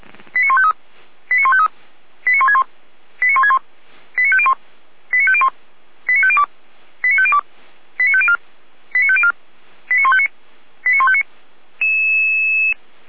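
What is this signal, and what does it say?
Five-tone selective call (ZVEI), heard as quick runs of five tones, tones of about 70 ms. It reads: 89153 89153 89161 89161 89501 89501 89502 89502 89504 89504 89159 89159
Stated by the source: ZVEI five-tone selective call 89099 received by SDR, from the Feuerwehr Orlamünde audio archive.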